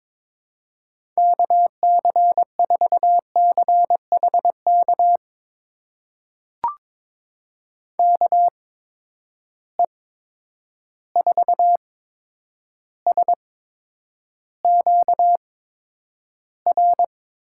Morse code, heard as a keyed tone, about 22 wpm, spaced as for 9 wpm, 700 Hz. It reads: KC4CHK KE4SQR
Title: USA Call Signs - 6 Characters - 8x Character Spacing - 22wpm